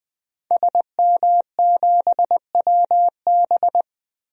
Morse code, heard as a keyed tone, about 20 wpm, 700 Hz. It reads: SM7WB
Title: DXCC-100 Call Signs - 5 Characters - Repeated 2x - 20wpm